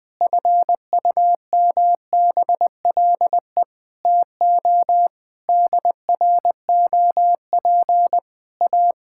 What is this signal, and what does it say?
Morse code, heard as a keyed tone, about 20 wpm, 700 Hz. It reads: FUMBLE TO DROP A
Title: Pangram Challenge v2 - 20wpm